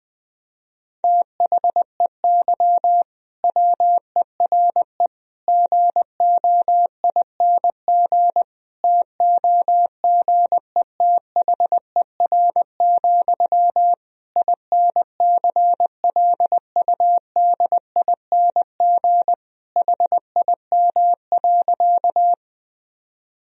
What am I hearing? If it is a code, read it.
Morse code, heard as a keyed tone, about 20 wpm, 700 Hz. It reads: THEY WERE GOING TOGETHER, INCLUDING HIM.
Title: Sentences from Top 1000 Words - Review Part 4 - 20wpm